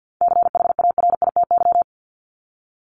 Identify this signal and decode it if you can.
Morse code, heard as a keyed tone, about 50 wpm, 700 Hz. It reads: CHARITY
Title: Top 1501-1750 Words 50wpm